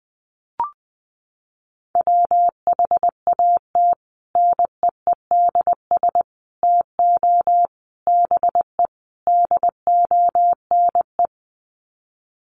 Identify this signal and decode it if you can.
Morse code, heard as a keyed tone, about 20 wpm, 700 Hz. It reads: WHAT NEEDS TO BE DONE
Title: Sets of 5 Words Encore 20wpm